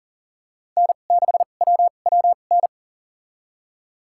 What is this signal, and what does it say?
Morse code, heard as a keyed tone, about 40 wpm, spaced as for 29 wpm, 700 Hz. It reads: N6WWN
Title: USA Call Signs - 5 Characters - 2x Character Spacing - 40wpm